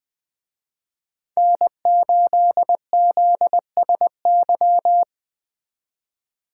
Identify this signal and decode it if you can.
Morse code, heard as a keyed tone, about 20 wpm, 700 Hz. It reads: N8ZSY